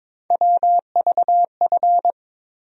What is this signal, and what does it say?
Morse code, heard as a keyed tone, about 22 wpm, 700 Hz. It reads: WVF
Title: USA Call Sign Postfixes - 3 Letters 22wpm